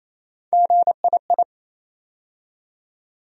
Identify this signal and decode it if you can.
Morse code, heard as a keyed tone, about 27 wpm, 700 Hz. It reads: GII
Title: USA Call Sign Postfixes - 3 Letters 28wpm